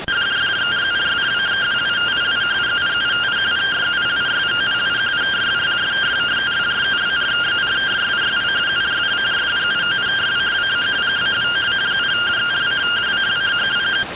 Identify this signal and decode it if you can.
Two-tone teleprinter signal, about 170 Hz shift.